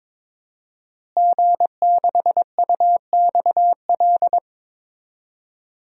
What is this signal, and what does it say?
Morse code, heard as a keyed tone, about 23 wpm, 700 Hz. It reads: G6UXL
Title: DXCC-100 Call Signs - 5 Characters - Repeated 3x - 22wpm